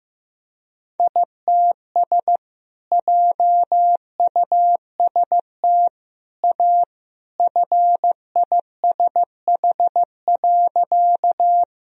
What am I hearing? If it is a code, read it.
Morse code, heard as a keyed tone, about 16 wpm, 700 Hz. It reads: ITS JUST A FISH.
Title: Sentences from Top 1000 Words - Review Part 1 - 15wpm